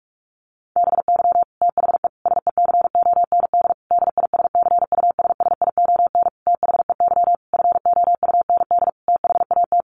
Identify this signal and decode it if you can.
Morse code, heard as a keyed tone, about 44 wpm, 700 Hz. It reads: BY THE SECOND DISCUSSION THEY FOUND THAT
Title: Sentences from Top 1200 Words 45wpm